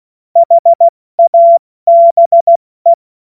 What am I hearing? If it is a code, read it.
Morse code, heard as a keyed tone, about 16 wpm, 680 Hz. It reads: HABE
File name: Witz-09-September-2025-70-F80